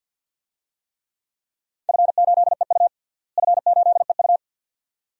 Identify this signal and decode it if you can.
Morse code, heard as a keyed tone, about 52 wpm, 700 Hz. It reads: F8EU F8EU
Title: DXCC-100 Call Signs - 4 Characters - Repeated 2x - 50wpm